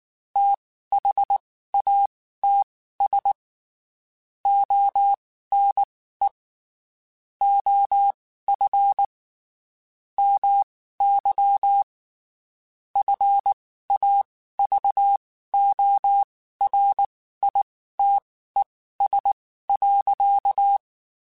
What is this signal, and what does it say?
Morse code, heard as a keyed tone, about 19 wpm, spaced as for 14 wpm, 800 Hz. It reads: THATS ONE OF MY FAVORITES.
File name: (spaced) / QSOs Spaced-mp3s / WD0FBT 1 @18 sp